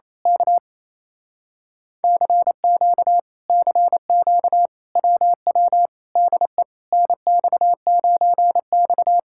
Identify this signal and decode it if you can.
Morse code, heard as a keyed tone, about 26 wpm, 700 Hz. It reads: K CQ CQ WW DE NX9X
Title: Speed-Racing Common QSO Phrases Encore - 28wpm